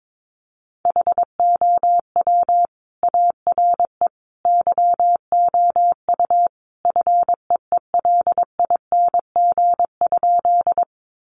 Morse code, heard as a keyed tone, about 22 wpm, 700 Hz. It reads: HOW ARE YOU FEELING?
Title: Sentences from Top 1200 Words 22wpm